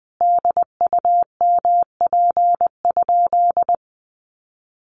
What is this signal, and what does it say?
Morse code, heard as a keyed tone, about 20 wpm, 700 Hz. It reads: DUMP?
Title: Sentences from Top 1300 Words 20wpm